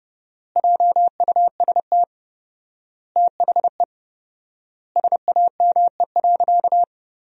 Morse code, heard as a keyed tone, about 30 wpm, 700 Hz. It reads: JUST THE SAME.
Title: 4x Word-Spacing - Sentences from Top 300 Words - 30wpm